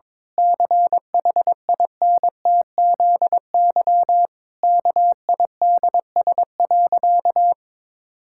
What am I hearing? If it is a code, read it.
Morse code, heard as a keyed tone, about 22 wpm, 700 Hz. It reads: CHINTZY KIDS.
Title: Pangram Challenge v2 - 22wpm